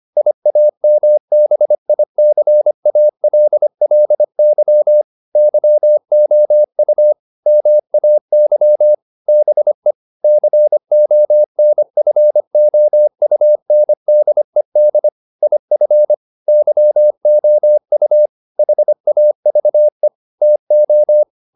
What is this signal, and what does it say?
Morse code, heard as a keyed tone, about 25 wpm, 600 Hz. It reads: IAMBICALLY YOU MAY BE CONFOUNDED IF YOU HAVE TO
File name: SS315 25